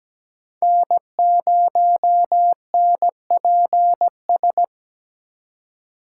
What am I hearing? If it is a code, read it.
Morse code, heard as a keyed tone, about 20 wpm, 700 Hz. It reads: N0NPS
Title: Speed Racing - USA Call Signs 17wpm